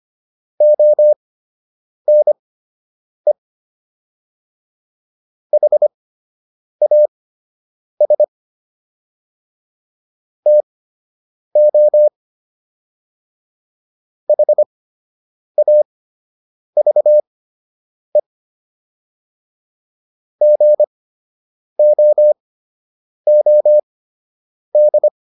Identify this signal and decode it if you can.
Morse code, heard as a keyed tone, about 25 wpm, spaced as for 8 wpm, 600 Hz. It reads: ONE HAS TO HAVE GOOD